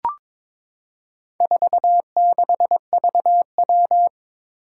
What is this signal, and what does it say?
Morse code, heard as a keyed tone, about 22 wpm, 700 Hz. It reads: 46VW
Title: Four Challenging Letter-Number Combinations 22wpm